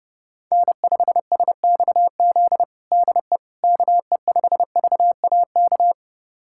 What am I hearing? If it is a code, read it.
Morse code, heard as a keyed tone, about 32 wpm, 700 Hz. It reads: N5SXZ DE KE5VAK
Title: USA Callsign DE Callsign 30wpm